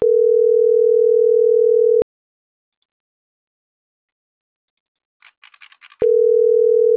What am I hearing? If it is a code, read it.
Ringback tone (North America), 440 + 480 Hz.